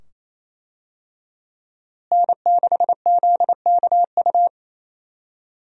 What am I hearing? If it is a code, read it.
Morse code, heard as a keyed tone, about 28 wpm, 700 Hz. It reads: N6ZKU
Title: USA Call Signs - 5 Characters - Repeated 2x - 28wpm